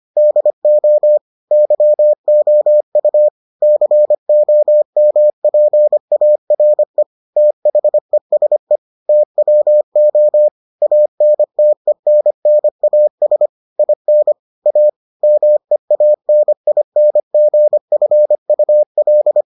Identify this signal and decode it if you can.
Morse code, heard as a keyed tone, about 25 wpm, 600 Hz. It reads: DO YOU COMPARE THESE TWO ANTENNAS IN A MEANINGFUL